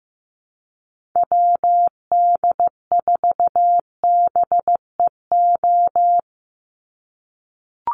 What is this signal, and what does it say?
Morse code, heard as a keyed tone, about 15 wpm, 700 Hz. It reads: WD4BEO
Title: USA Call Signs 15wpm